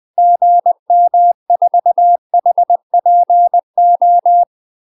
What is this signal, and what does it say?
Morse code, heard as a keyed tone, about 20 wpm, 700 Hz. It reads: GM4HPO
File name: UK MASTER QSOs TRAINING / 20WPM / UK MASTER QSOs 0050